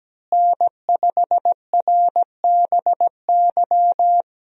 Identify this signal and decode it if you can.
Morse code, heard as a keyed tone, about 17 wpm, 700 Hz. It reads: N5RBY
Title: USA Call Signs - 5 Characters - Repeated 3x - 17wpm